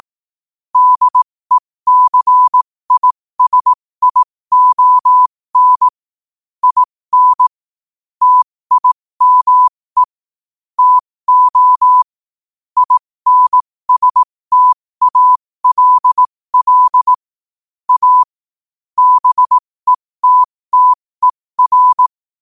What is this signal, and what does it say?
Morse code, heard as a keyed tone, about 18 wpm, spaced as for 15 wpm, 1000 Hz. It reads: DECISION IN TIME TO INSTALL A BETTER
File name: qotdfiles / 15wpm